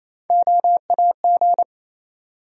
Morse code, heard as a keyed tone, about 28 wpm, 700 Hz. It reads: OAG
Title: USA Call Sign Postfixes - 3 Letters 28wpm